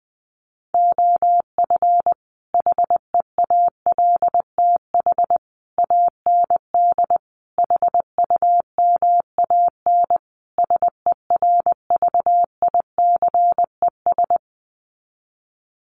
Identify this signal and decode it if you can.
Morse code, heard as a keyed tone, about 20 wpm, 700 Hz. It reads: OF HEALTH AND HUMAN SERVICES
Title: Sets of 5 Words 20wpm